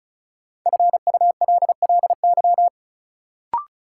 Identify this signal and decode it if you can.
Morse code, heard as a keyed tone, about 34 wpm, 700 Hz. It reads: FULLY
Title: Sentences from Top 1300 Words - New Word Drill Alt - 35wpm